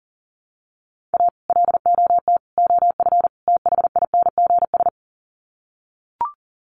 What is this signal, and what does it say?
Morse code, heard as a keyed tone, about 37 wpm, 700 Hz. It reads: A LOT OF THINGS
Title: Sets of 4 Words 40wpm